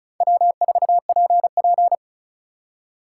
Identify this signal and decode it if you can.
Morse code, heard as a keyed tone, about 43 wpm, 700 Hz. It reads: W4PP
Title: Speed Racing - USA 1x2 Call Signs 35wpm